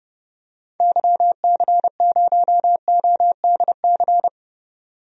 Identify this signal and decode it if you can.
Morse code, heard as a keyed tone, about 32 wpm, 700 Hz. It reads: YC0ODC
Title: DXCC-100 Call Signs - 6 Characters - 30wpm